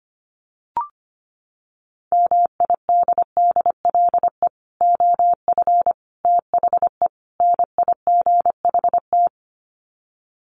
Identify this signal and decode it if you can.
Morse code, heard as a keyed tone, about 25 wpm, 700 Hz. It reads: MIDDLE OF THE NIGHT
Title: Sets of 4 Words 25wpm